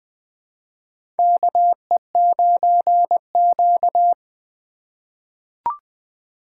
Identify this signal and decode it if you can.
Morse code, heard as a keyed tone, about 20 wpm, 700 Hz. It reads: KE9Q
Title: USA Call Signs - 4 Characters - 2x Character Spacing - 20wpm